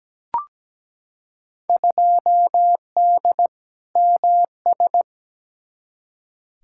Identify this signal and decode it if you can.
Morse code, heard as a keyed tone, about 17 wpm, 700 Hz. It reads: 2D MS
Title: Field Day Exchanges Encore - 17wpm